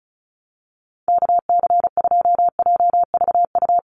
Morse code, heard as a keyed tone, about 34 wpm, 700 Hz. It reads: KC2JVU
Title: USA Call Signs 35wpm